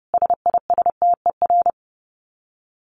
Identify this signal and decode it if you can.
Morse code, heard as a keyed tone, about 30 wpm, 700 Hz. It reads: SISTER
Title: Dit Challenge 30wpm